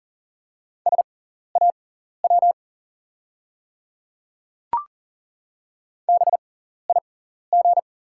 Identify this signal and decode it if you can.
Morse code, heard as a keyed tone, about 40 wpm, spaced as for 14 wpm, 690 Hz. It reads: SAW BIG
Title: 6x Character Spacing - Top 300 Words - 40wpm